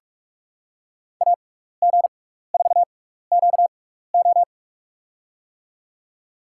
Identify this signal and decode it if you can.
Morse code, heard as a keyed tone, about 42 wpm, spaced as for 15 wpm, 700 Hz. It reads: AG4QO
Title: USA Call Signs - 5 Characters - 6x Character Spacing - 45wpm